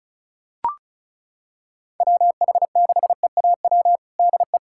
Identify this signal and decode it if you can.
Morse code, heard as a keyed tone, about 36 wpm, 700 Hz. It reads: WH6EAW DE
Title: USA Callsign DE Callsign 35wpm